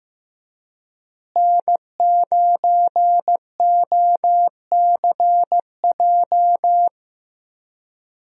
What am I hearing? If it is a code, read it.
Morse code, heard as a keyed tone, about 15 wpm, 700 Hz. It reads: N9OCJ
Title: USA Call Signs - 5 Characters - Repeated 3x - 15wpm